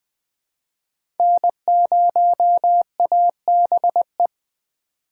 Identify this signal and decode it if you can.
Morse code, heard as a keyed tone, about 20 wpm, 700 Hz. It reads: N0ABE
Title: USA Call Signs - 5 Characters - 3x Character Spacing - 20wpm